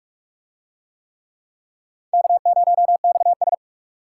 Morse code, heard as a keyed tone, about 44 wpm, 700 Hz. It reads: K0XS